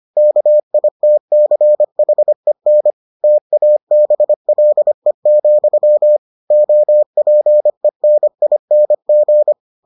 Morse code, heard as a keyed tone, about 25 wpm, 600 Hz. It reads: KITCHEN TABLE, OPENING